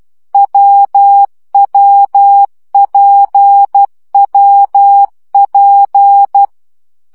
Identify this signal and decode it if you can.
Morse code, heard as a keyed tone, about 12 wpm, 800 Hz. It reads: WWPWP
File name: CW gyakorlas w p j 10 baud